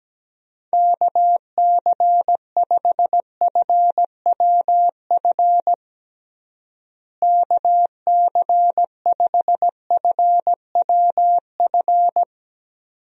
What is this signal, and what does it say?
Morse code, heard as a keyed tone, about 17 wpm, 700 Hz. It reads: KC5FWF KC5FWF
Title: USA Call Signs - 6 Characters - Repeated 2x - 17wpm